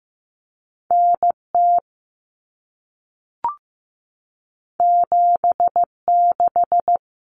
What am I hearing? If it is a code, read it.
Morse code, heard as a keyed tone, about 15 wpm, 700 Hz. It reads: NT 76
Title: Common Cut Numbers - 2-Digit 15wpm